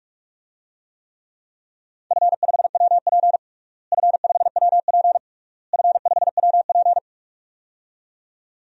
Morse code, heard as a keyed tone, about 46 wpm, 700 Hz. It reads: F5WP F5WP F5WP
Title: DXCC-100 Call Signs - 4 Characters - Repeated 3x - 45wpm